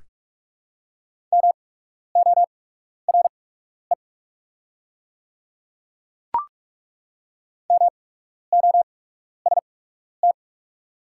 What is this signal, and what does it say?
Morse code, heard as a keyed tone, about 48 wpm, spaced as for 12 wpm, 700 Hz. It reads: MORE MOST
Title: 8x Character Spacing - Top 200 Words - 45wpm